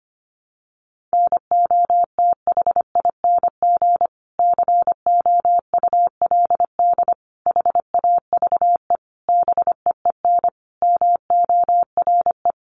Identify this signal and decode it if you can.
Morse code, heard as a keyed tone, about 25 wpm, 700 Hz. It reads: NOTHING COULD HAVE BEEN MORE
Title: Sentences from Top 1300 Words 25wpm